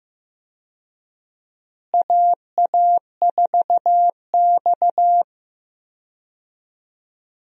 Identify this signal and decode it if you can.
Morse code, heard as a keyed tone, about 15 wpm, 700 Hz. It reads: AA4X
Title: USA Call Signs - 4 Characters - Repeated 3x - 15wpm